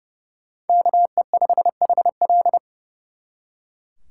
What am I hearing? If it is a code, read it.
Morse code, heard as a keyed tone, about 37 wpm, 700 Hz. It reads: KE5HL
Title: Speed Racing - USA 2x2 Call Signs 30wpm